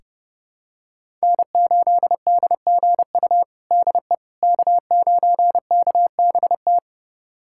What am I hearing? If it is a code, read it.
Morse code, heard as a keyed tone, about 32 wpm, 700 Hz. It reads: N8DGU DE K9KBT
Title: USA Callsign DE Callsign - 5 Characters - 30wpm